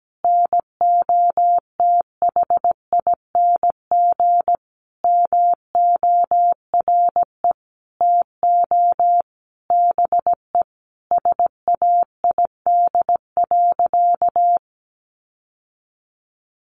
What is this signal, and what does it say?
Morse code, heard as a keyed tone, about 17 wpm, 700 Hz. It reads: NOTHING MORE TO BE SAID.
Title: Sentences from Top 1000 Words 17wpm